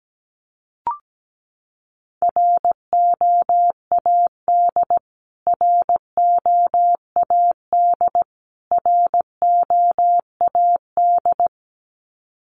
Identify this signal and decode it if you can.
Morse code, heard as a keyed tone, about 17 wpm, 700 Hz. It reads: ROAD ROAD ROAD
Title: Top 500 Words Repeated 17wpm